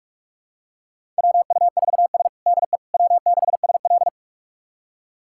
Morse code, heard as a keyed tone, about 46 wpm, 700 Hz. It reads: WU4S DE W6SL
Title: USA Callsign DE Callsign - 4 Characters - 45wpm